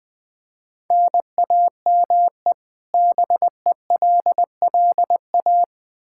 Morse code, heard as a keyed tone, about 21 wpm, 700 Hz. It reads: NAME BELLA
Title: Practice QSO - Exchange 1 - Beginning - 20wpm